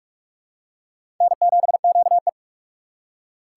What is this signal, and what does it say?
Morse code, heard as a keyed tone, about 46 wpm, 700 Hz. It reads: N7QE
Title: USA Call Signs - 4 Characters - 6x Character Spacing - 45wpm